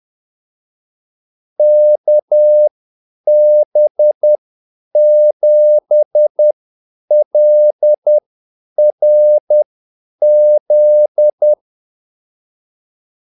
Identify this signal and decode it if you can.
Morse code, heard as a keyed tone, about 10 wpm, spaced as for 8 wpm, 600 Hz. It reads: KB7LRZ
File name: Callsigns-10-8wpm